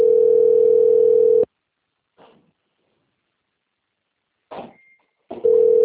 Ringback tone (North America), 440 + 480 Hz.